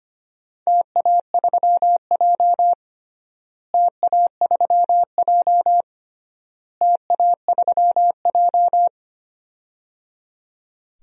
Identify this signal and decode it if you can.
Morse code, heard as a keyed tone, about 25 wpm, 700 Hz. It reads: TA3J TA3J TA3J
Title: DXCC-100 Call Signs - 4 Characters - Repeated 3x - 25wpm